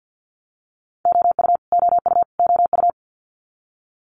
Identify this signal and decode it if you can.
Morse code, heard as a keyed tone, about 50 wpm, 700 Hz. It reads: OU OU OU
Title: Multi-letter Phonemes Repeated 3x 50wpm